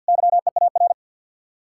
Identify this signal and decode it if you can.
Morse code, heard as a keyed tone, about 66 wpm, spaced as for 57 wpm, 700 Hz. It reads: YEAR
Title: Speed-Racing Common QSO Phrases Encore - 50wpm